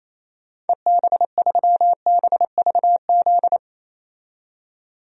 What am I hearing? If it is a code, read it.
Morse code, heard as a keyed tone, about 28 wpm, 700 Hz. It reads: EB3BVZ
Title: DXCC-100 Call Signs - 6 Characters - 28wpm